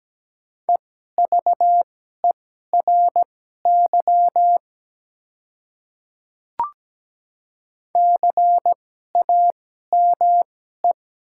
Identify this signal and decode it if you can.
Morse code, heard as a keyed tone, about 17 wpm, spaced as for 12 wpm, 700 Hz. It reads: EVERY CAME
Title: 2x Character Spacing - Top 200 Words - 17wpm